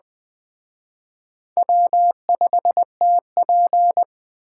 Morse code, heard as a keyed tone, about 21 wpm, 700 Hz. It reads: W5TP